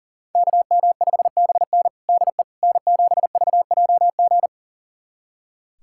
Morse code, heard as a keyed tone, about 40 wpm, 700 Hz. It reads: KM5BN DE N7VJG
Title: USA Callsign DE Callsign - 5 Characters - 40wpm